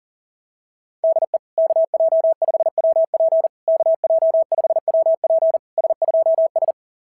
Morse code, heard as a keyed tone, about 38 wpm, 690 Hz. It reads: DE KJ5WP KJ5WP S2S
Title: SOTA QSO Practice 40wpm